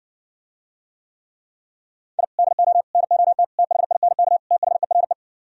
Morse code, heard as a keyed tone, about 59 wpm, 700 Hz. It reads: I DO NOT THINK THERE